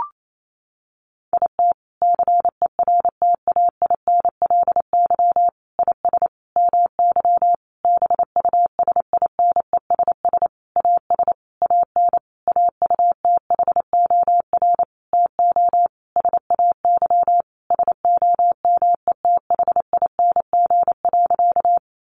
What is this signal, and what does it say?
Morse code, heard as a keyed tone, about 28 wpm, 700 Hz. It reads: IT CERTAINLY IS MY BUSINESS AS AN AUTHOR TO SAY SOMETHING.
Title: Sentences from Top 1200 Words 28wpm